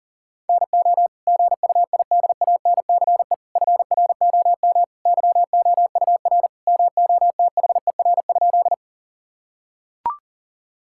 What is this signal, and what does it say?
Morse code, heard as a keyed tone, about 40 wpm, 700 Hz. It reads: NO GUIDANCE FROM YOUR MOTHER?
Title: Sentences from Top 1600 Words 40wpm